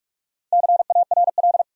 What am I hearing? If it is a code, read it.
Morse code, heard as a keyed tone, about 46 wpm, 700 Hz. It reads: CARL
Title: USA Common Names - Max 5 Letters - 45wpm